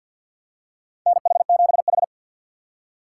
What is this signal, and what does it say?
Morse code, heard as a keyed tone, about 49 wpm, 690 Hz. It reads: NH7H
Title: USA Call Signs - 4 Characters - 6x Character Spacing - 50wpm